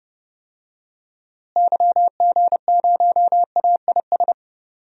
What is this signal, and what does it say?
Morse code, heard as a keyed tone, about 30 wpm, 700 Hz. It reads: YG0AIS
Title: DXCC-100 Call Signs - 6 Characters - Repeated 3x - 30wpm